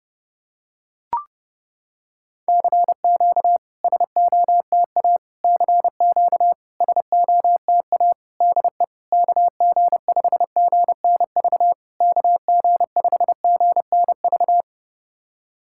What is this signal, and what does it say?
Morse code, heard as a keyed tone, about 30 wpm, 700 Hz. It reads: CQ SOTA CQ SOTA DE KG5GNV KG5GNV
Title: SOTA QSO Practice 30wpm